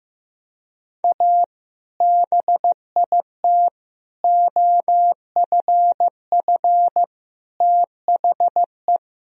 Morse code, heard as a keyed tone, about 15 wpm, 700 Hz. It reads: A BIT OFF THE
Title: Sentences from Top 1000 Words - Review Part 2 - 15wpm